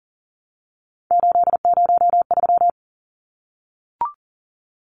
Morse code, heard as a keyed tone, about 40 wpm, 700 Hz. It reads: 803
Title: Numbers - 3-Digit 40wpm